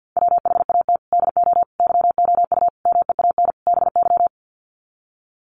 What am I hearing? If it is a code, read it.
Morse code, heard as a keyed tone, about 50 wpm, 700 Hz. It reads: WHAT DO YOU MEAN BY
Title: Sets of 5 Words 50wpm